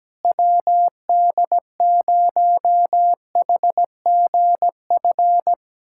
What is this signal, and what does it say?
Morse code, heard as a keyed tone, about 17 wpm, 700 Hz. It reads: WD0HGF
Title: USA Callsign DE Callsign - 6 Characters - 17wpm